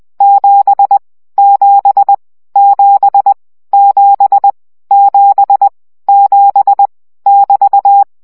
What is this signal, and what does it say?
Morse code, heard as a keyed tone, about 20 wpm, 800 Hz. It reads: 7 7 7 7 7 7 <BT>
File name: CW gyakorlas 7 17 baud